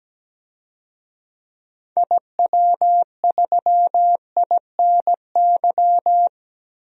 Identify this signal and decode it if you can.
Morse code, heard as a keyed tone, about 17 wpm, 700 Hz. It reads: IW3INY
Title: DXCC-100 Call Signs - 6 Characters - 17wpm